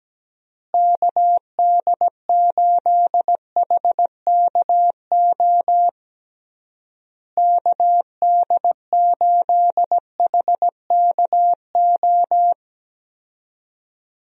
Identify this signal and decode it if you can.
Morse code, heard as a keyed tone, about 17 wpm, 700 Hz. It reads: KD8HKO KD8HKO
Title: USA Call Signs - 6 Characters - Repeated 3x - 17wpm